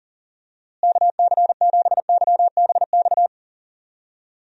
Morse code, heard as a keyed tone, about 37 wpm, 700 Hz. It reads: KC7YBX
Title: USA Call Signs - 6 Characters - Repeated 3x - 40wpm